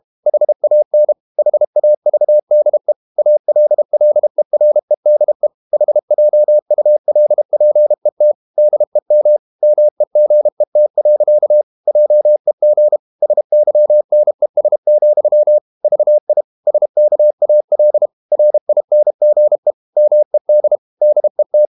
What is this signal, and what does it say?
Morse code, heard as a keyed tone, about 32 wpm, 600 Hz. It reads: HAN HAVDE ALLEREDE HJULPET DEM MEGET. JEG SYNES, VI SKAL RINGE MED DET
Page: Ferie 32 wpm